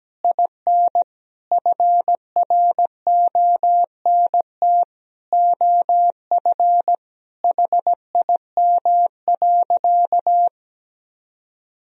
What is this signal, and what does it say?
Morse code, heard as a keyed tone, about 17 wpm, 700 Hz. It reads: IN FRONT OF HIM.
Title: Sentences from Top 1000 Words - Review Part 1 - 17wpm